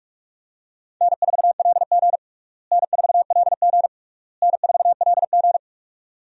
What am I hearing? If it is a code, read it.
Morse code, heard as a keyed tone, about 46 wpm, 700 Hz. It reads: N4LG N4LG N4LG
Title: USA Call Signs - 4 Characters - Repeated 3x - 45wpm